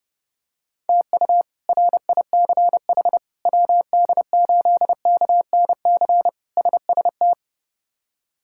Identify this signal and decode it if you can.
Morse code, heard as a keyed tone, about 30 wpm, 700 Hz. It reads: TU RICH WD8KNC SST
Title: K1USN SST QSO Practice - 30wpm